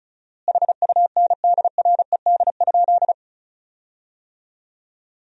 Morse code, heard as a keyed tone, about 36 wpm, 700 Hz. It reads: HUNDRED?
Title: 4x Word-Spacing - Sentences from Top 400 Words - 35wpm